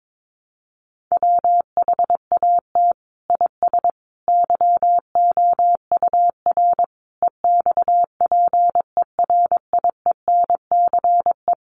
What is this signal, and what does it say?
Morse code, heard as a keyed tone, about 22 wpm, 700 Hz. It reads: WHAT IS YOUR EXPERIENCE
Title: Sentences from Top 600 Words 22wpm